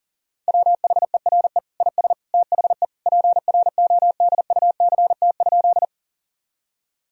Morse code, heard as a keyed tone, about 40 wpm, 700 Hz. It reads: WHERE IS THE PRODUCT?
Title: Sentences from Top 1000 Words - Review Part 2 - 40wpm